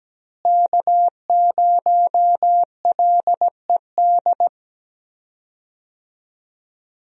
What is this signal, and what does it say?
Morse code, heard as a keyed tone, about 17 wpm, 700 Hz. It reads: K0LED
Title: USA Call Signs - 5 Characters - Repeated 2x - 17wpm